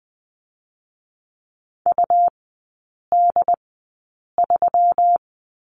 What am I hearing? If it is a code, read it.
Morse code, heard as a keyed tone, about 20 wpm, 700 Hz. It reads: U D 3
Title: 121 Lesson 20e - Number 3 - Three Characters 20wpm